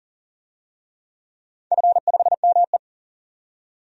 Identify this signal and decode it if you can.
Morse code, heard as a keyed tone, about 40 wpm, 700 Hz. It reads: F5ME
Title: DXCC-100 Call Signs - 4 Characters - 40wpm